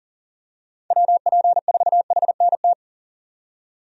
Morse code, heard as a keyed tone, about 40 wpm, 700 Hz. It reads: WP4HNT
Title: USA Call Signs - 6 Characters - 40wpm